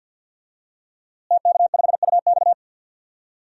Morse code, heard as a keyed tone, about 52 wpm, 700 Hz. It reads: TK5UX
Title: DXCC-100 Call Signs - 5 Characters - Repeated 3x - 50wpm